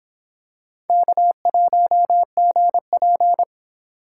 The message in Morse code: K1GP

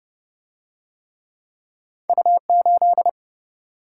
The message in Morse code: U8

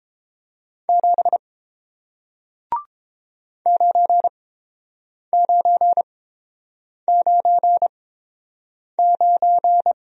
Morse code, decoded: 7 9999

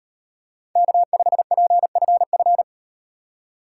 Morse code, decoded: K5PFF

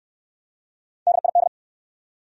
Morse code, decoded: DID

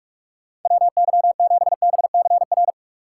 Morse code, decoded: WY7BCR

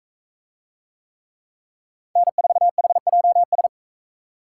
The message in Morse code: N4HJS

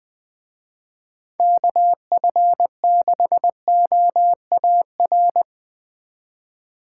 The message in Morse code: KF6OAR

